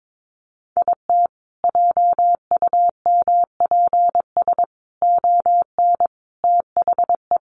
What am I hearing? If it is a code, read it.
Morse code, heard as a keyed tone, about 22 wpm, 700 Hz. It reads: IT JUMPS ON THE
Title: Additional Easy Phrases 22wpm